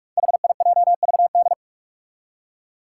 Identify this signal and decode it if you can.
Morse code, heard as a keyed tone, about 44 wpm, 700 Hz. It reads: HIJVD
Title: Five Challenging Letter-Number Combinations 45wpm